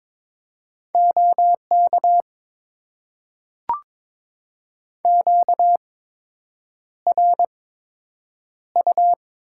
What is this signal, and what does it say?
Morse code, heard as a keyed tone, about 22 wpm, spaced as for 8 wpm, 700 Hz. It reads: OK QRU
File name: Top 90 QSO Elements - 8x Character Spacing - 22wpm